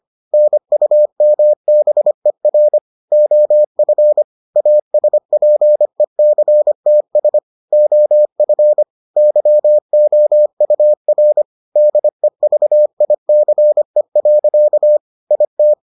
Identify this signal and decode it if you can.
Morse code, heard as a keyed tone, about 25 wpm, 600 Hz. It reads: NUMBER OF ASPECTS OF YOUR DEVICE. IT